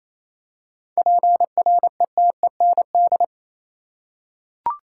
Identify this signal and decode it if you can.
Morse code, heard as a keyed tone, about 27 wpm, 700 Hz. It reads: PRETEND
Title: Seven Letter Words - 28wpm